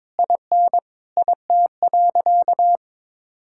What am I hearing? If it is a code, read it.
Morse code, heard as a keyed tone, about 22 wpm, 700 Hz. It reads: IN IT.
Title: Sentences from Top 1600 Words 22wpm